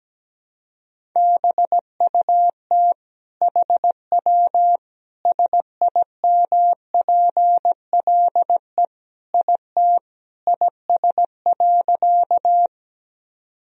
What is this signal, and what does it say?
Morse code, heard as a keyed tone, about 17 wpm, 700 Hz. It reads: BUT HW SIMPLE IT IS.
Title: Sentences from Top 400 Words and Abbreviations 17wpm